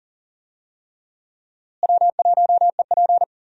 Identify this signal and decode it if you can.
Morse code, heard as a keyed tone, about 39 wpm, 700 Hz. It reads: W1EP
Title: USA Call Signs - 4 Characters - Repeated 3x - 40wpm